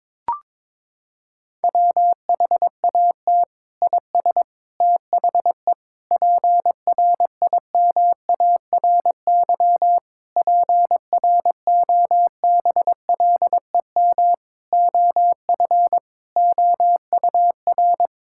Morse code, heard as a keyed tone, about 22 wpm, 700 Hz. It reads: WHAT IS THE PRIMARY PROBLEM OF OUR
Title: Sentences from Top 1000 Words - Review Part 5 - 22wpm